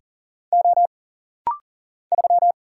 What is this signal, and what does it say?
Morse code, heard as a keyed tone, about 40 wpm, 700 Hz. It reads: O 3
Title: Single Letter-Number Alt - Rapid-Fire 40wpm